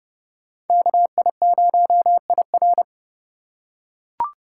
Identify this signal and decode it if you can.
Morse code, heard as a keyed tone, about 27 wpm, 700 Hz. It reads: KI0IR